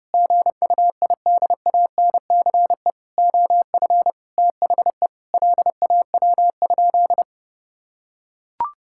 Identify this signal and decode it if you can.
Morse code, heard as a keyed tone, about 30 wpm, 700 Hz. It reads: GUIDANCE OF THE LAW?
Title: Sentences from Top 1600 Words 30wpm